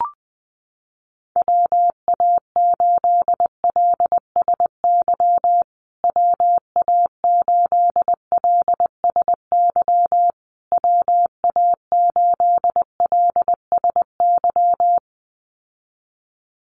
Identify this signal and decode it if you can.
Morse code, heard as a keyed tone, about 20 wpm, 700 Hz. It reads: WA8LSY WA8LSY WA8LSY
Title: USA Call Signs Repeated 3x 20wpm